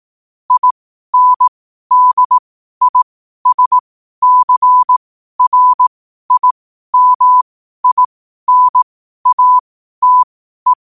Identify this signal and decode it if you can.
Morse code, heard as a keyed tone, about 18 wpm, spaced as for 12 wpm, 1000 Hz. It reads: INDISCRIMINATE
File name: qotdfiles / 12wpm